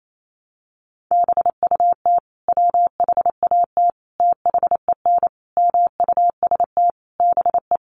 Morse code, heard as a keyed tone, about 28 wpm, 700 Hz. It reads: BUT WHAT THEN MUST BE
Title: Sentences from Top 500 Words 28wpm